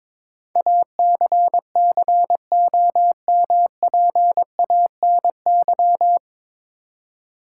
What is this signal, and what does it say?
Morse code, heard as a keyed tone, about 22 wpm, 700 Hz. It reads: ACCOMPANY
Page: Morse Code Ninja practice 22wpm